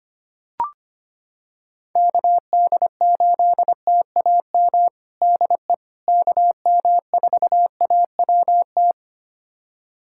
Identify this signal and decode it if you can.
Morse code, heard as a keyed tone, about 25 wpm, 700 Hz. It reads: KD8TAM DE KM4AWT
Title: USA Callsign DE Callsign - 6 Characters - 25wpm